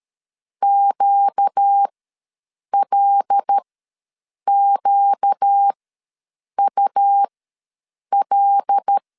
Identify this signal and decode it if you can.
Morse code, heard as a keyed tone, about 13 wpm, spaced as for 7 wpm, 800 Hz. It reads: QLQUL
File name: LEZIONI / 5 LBVC / 5 L ws 12wpm 2958